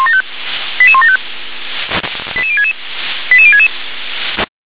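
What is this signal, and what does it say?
Five-tone selective call (ZVEI), heard as quick runs of five tones, tones of about 70 ms. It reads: 88175 88066 88066
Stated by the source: ZVEI five-tone selective call received on SDR.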